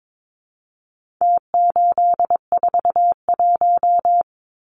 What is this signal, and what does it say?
Morse code, heard as a keyed tone, about 22 wpm, 700 Hz. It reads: T841
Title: Numbers - Contest Style - 4-Digit 22wpm